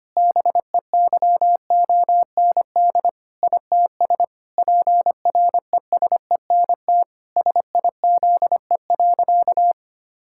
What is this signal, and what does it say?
Morse code, heard as a keyed tone, about 25 wpm, 700 Hz. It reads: BEYOND ITS PRESENT SIZE.
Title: Sentences from Top 1000 Words - Review Part 6 - 25wpm